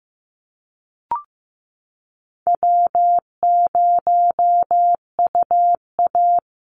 Morse code, heard as a keyed tone, about 14 wpm, 700 Hz. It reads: W0UA
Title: Easy Call Signs Repeated 2x - 15wpm